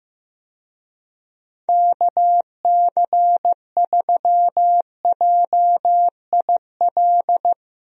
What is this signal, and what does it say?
Morse code, heard as a keyed tone, about 16 wpm, 700 Hz. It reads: KC3JIL
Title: USA Call Signs - 6 Characters - 15wpm